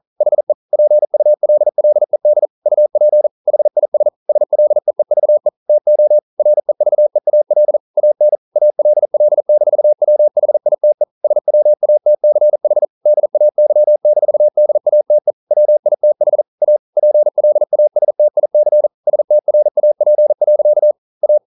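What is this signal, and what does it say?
Morse code, heard as a keyed tone, about 40 wpm, 600 Hz. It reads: HE PULLED UP HIS SLEEVE TO REVEAL AN ALL-WHITE SWATCH DAY-DATE WITH A PLASTIC STRAP. A
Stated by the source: Morse code practice at 41 wpm, from CW med Gnister.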